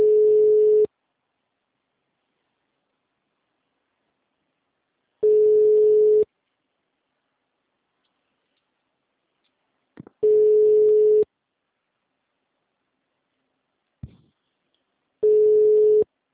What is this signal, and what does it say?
Ringback tone (Europe, 425 Hz).